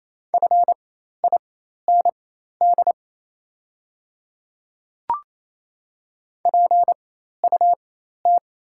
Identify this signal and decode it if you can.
Morse code, heard as a keyed tone, about 28 wpm, spaced as for 13 wpm, 690 Hz. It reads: FIND PUT
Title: 4x Character Spacing - Top 500 Words - 28wpm